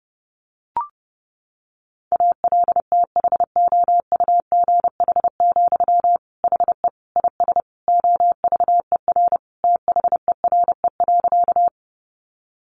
Morse code, heard as a keyed tone, about 30 wpm, 700 Hz. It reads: ALTHOUGH, HE IS OVER THERE.
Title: Sentences from Top 600 Words 30wpm